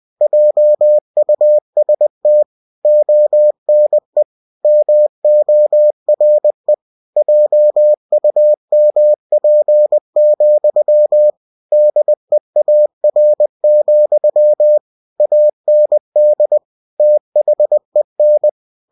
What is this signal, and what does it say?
Morse code, heard as a keyed tone, about 20 wpm, 600 Hz. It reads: JUST ONE MORE JUMP, DEAR, AND THEN
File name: cw-books / winnie-the-pooh / winnie-20wpm-page-0066